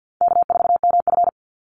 Morse code, heard as a keyed tone, about 50 wpm, 700 Hz. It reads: K4MF
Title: NA SOTA Chaser Callsigns 50wpm